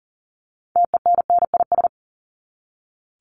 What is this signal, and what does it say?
Morse code, heard as a keyed tone, about 40 wpm, 700 Hz. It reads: TENNIS
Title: Top 1501-1750 Words 40wpm